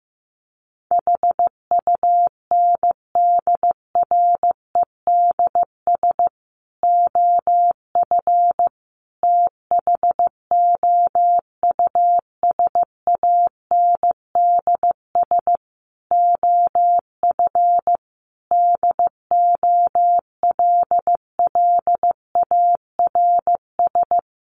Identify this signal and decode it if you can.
Morse code, heard as a keyed tone, about 15 wpm, 700 Hz. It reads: HUNDREDS OF THOUSANDS OF DOLLARS
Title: Sets of 5 Words Encore 15wpm